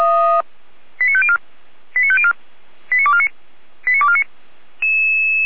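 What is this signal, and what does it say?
Five-tone selective call (ZVEI), heard as quick runs of five tones, tones of about 70 ms. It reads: 89504 89504 89249 89249
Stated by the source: ZVEI five-tone selective call received with an SDR, from the Feuerwehr Orlamünde fire brigade server.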